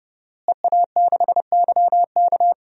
Morse code, heard as a keyed tone, about 30 wpm, 700 Hz. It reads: EA6YK
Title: DXCC-100 Call Signs - 5 Characters - Repeated 3x - 30wpm